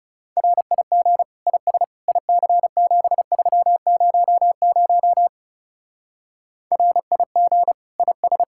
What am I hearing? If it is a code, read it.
Morse code, heard as a keyed tone, about 38 wpm, 700 Hz. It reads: RIG IS IC7300 RIG IS